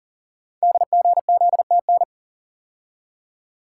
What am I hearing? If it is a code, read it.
Morse code, heard as a keyed tone, about 44 wpm, 690 Hz. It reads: DGZTN
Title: Code Groups - 5 Letter:Numbers - 40wpm